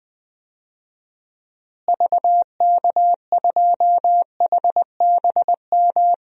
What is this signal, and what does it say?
Morse code, heard as a keyed tone, about 20 wpm, 700 Hz. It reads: VK2HBM
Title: DXCC-100 Call Signs - 6 Characters - Repeated 3x - 20wpm